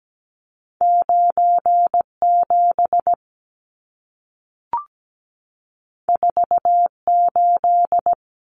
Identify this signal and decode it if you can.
Morse code, heard as a keyed tone, about 17 wpm, 700 Hz. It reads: 97 48
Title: Numbers - 2-Digit 17wpm